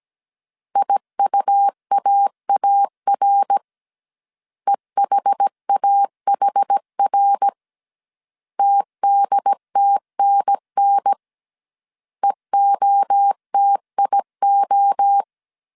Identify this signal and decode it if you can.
Morse code, heard as a keyed tone, about 17 wpm, 800 Hz. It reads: IUAAR EHAHR TDTNN EOTIO